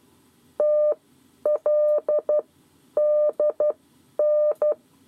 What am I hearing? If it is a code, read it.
Morse code, heard as a keyed tone, about 12 wpm, spaced as for 9 wpm, 600 Hz. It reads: TLDN
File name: 7 Lezione cw 2